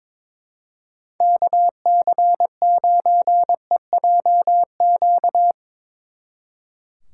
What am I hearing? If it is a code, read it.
Morse code, heard as a keyed tone, about 24 wpm, 700 Hz. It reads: KC9EJQ
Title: Speed Racing - USA Call Signs 22wpm